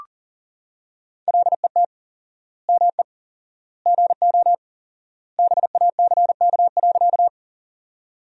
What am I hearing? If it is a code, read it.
Morse code, heard as a keyed tone, about 40 wpm, 700 Hz. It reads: LET ME GO BACK.
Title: 4x Word-Spacing - Sentences from Top 300 Words - 40wpm